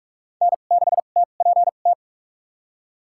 Morse code, heard as a keyed tone, about 46 wpm, spaced as for 33 wpm, 700 Hz. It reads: N6TPT
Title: USA Call Signs - 5 Characters - 2x Character Spacing - 45wpm